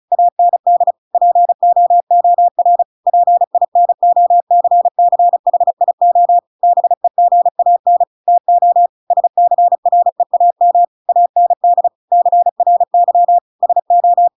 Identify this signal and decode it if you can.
Morse code, heard as a keyed tone, about 34 wpm, 700 Hz. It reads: AND POOR PINOCCHIO BEGAN TO SCREAM AND CRY SO
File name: Pinocchio / Pinocchio ch7 35 35 ik1hge